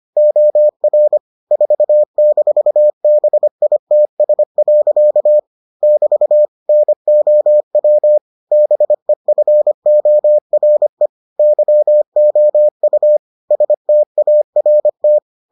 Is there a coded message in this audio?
Yes: Morse code from 0.2 to 15.2 s, heard as a keyed tone, about 25 wpm, 600 Hz. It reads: OR 4-BITS. <BT> NOW BEFORE YOU START